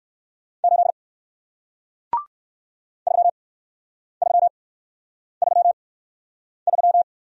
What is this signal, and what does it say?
Morse code, heard as a keyed tone, about 57 wpm, spaced as for 9 wpm, 700 Hz. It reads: 8 3333